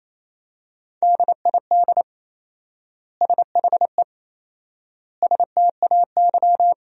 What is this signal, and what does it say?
Morse code, heard as a keyed tone, about 28 wpm, 700 Hz. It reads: DID SHE STAY
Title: 4x Word-Spacing - Sentences from Top 500 Words - 28wpm